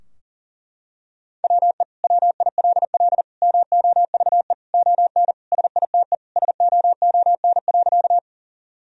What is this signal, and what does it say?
Morse code, heard as a keyed tone, about 40 wpm, 700 Hz. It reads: WE WILL MOVE ON SITE SOON.
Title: Sentences from Top 1000 Words - Review Part 4 - 40wpm